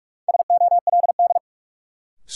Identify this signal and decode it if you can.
Morse code, heard as a keyed tone, about 46 wpm, 700 Hz. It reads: SOLD
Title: Sentences from Top 1400 Words - New Word Drill - 45wpm